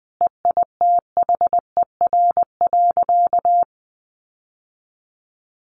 Morse code, heard as a keyed tone, about 20 wpm, 700 Hz. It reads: EITHER.
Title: Sentences from Top 600 Words 20wpm